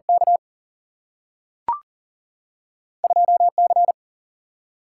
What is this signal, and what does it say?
Morse code, heard as a keyed tone, about 39 wpm, 700 Hz. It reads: K 2 C
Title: Single-Letter Single-Number 40wpm